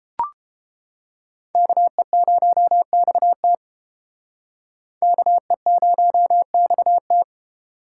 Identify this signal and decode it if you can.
Morse code, heard as a keyed tone, about 30 wpm, 700 Hz. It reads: KE0XT KE0XT